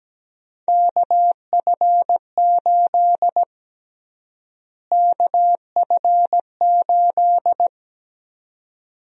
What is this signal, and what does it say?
Morse code, heard as a keyed tone, about 17 wpm, 700 Hz. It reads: KF8 KF8
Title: USA Call Sign Prefixes - Repeated 2x - 17wpm